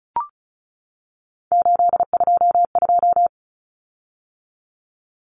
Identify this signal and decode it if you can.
Morse code, heard as a keyed tone, about 34 wpm, 700 Hz. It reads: 822
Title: Common Cut Numbers - 3-Digit 35wpm